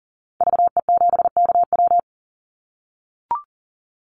Morse code, heard as a keyed tone, about 40 wpm, 700 Hz. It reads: VE7KW
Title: Call Signs - 40wpm